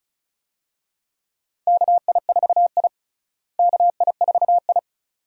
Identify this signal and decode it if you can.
Morse code, heard as a keyed tone, about 36 wpm, 700 Hz. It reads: KI4I KI4I
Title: USA Call Signs - 4 Characters - Repeated 2x - 35wpm